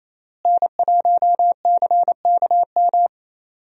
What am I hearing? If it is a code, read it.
Morse code, heard as a keyed tone, about 27 wpm, 700 Hz. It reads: N1CKM
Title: USA Call Signs - 5 Characters - Repeated 3x - 28wpm